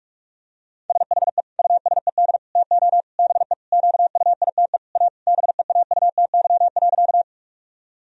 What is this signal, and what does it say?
Morse code, heard as a keyed tone, about 41 wpm, 700 Hz. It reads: SHE USED TO BE QUITE A BEAUTY.
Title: Sentences from Top 1000 Words - Review Part 2 - 45wpm